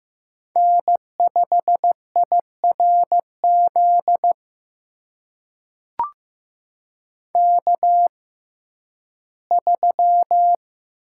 Morse code, heard as a keyed tone, about 15 wpm, spaced as for 11 wpm, 700 Hz. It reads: N5IRZ K3
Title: USA Call Signs - 5 Characters - 6x Character Spacing - 15wpm